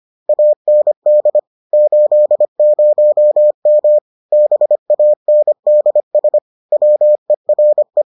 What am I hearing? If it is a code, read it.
Morse code, heard as a keyed tone, about 25 wpm, 600 Hz. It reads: AND 80M BANDS WERE